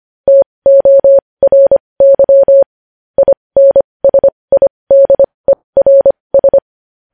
Morse code, heard as a keyed tone, about 25 wpm, spaced as for 20 wpm, 550 Hz. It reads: TORY INSIDERS
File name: podcast / News headlines-1x-25x20